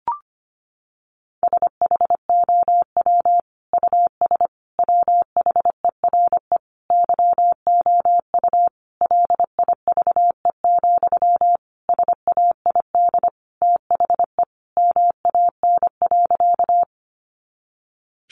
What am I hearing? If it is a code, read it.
Morse code, heard as a keyed tone, about 25 wpm, 700 Hz. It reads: SHOW US WHERE YOU LIVE, SAID THE MAN.
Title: Sentences from Top 200 Words 25wpm